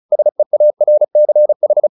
Morse code, heard as a keyed tone, about 34 wpm, 600 Hz. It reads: SEARCH
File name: the old man and the sea 2 35wpm